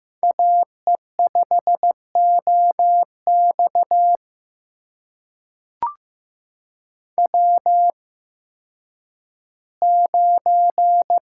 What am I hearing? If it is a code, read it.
Morse code, heard as a keyed tone, about 15 wpm, spaced as for 10 wpm, 700 Hz. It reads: AE5OX W9
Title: USA Call Signs - 5 Characters - 8x Character Spacing - 15wpm